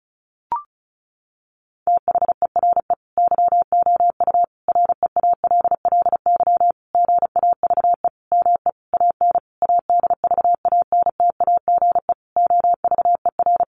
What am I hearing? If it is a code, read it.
Morse code, heard as a keyed tone, about 34 wpm, 700 Hz. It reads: THERE YOU REALLY GAVE ME AN ADVANTAGE OVER
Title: Sentences from Top 900 Words 35wpm